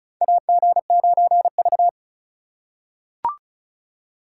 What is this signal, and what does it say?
Morse code, heard as a keyed tone, about 36 wpm, 700 Hz. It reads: AG9V
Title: USA Call Signs - 4 Characters - 4x Character Spacing - 35wpm